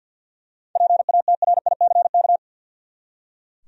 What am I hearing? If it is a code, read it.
Morse code, heard as a keyed tone, about 46 wpm, 700 Hz. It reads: PATRICK
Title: USA Common Names - 50wpm